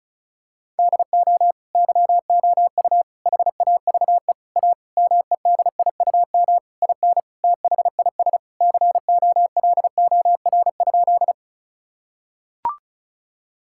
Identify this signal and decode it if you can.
Morse code, heard as a keyed tone, about 36 wpm, 700 Hz. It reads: DO YOU HAVE A MEDIUM IN THIS COLOR?
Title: Sentences from Top 1000 Words - Review Part 6 - 35wpm